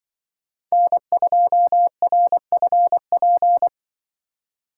Morse code, heard as a keyed tone, about 24 wpm, 700 Hz. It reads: N2RFP
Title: Speed Racing - USA 1x3 Call Signs 20wpm